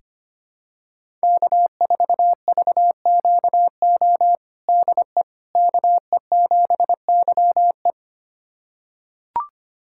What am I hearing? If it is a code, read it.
Morse code, heard as a keyed tone, about 25 wpm, 700 Hz. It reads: K4VQO DE KE7YE
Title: USA Callsign DE Callsign - 5 Characters - 25wpm